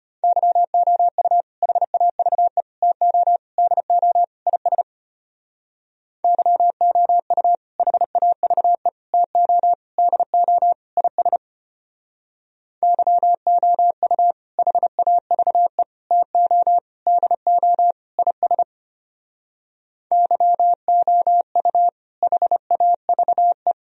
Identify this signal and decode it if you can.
Morse code, heard as a keyed tone, about 30 wpm, 700 Hz. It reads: YOU HAVE TO DO IS YOU HAVE TO DO IS YOU HAVE TO DO IS YOU HAVE
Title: Speed-Racing - Sets of 5 Words 25wpm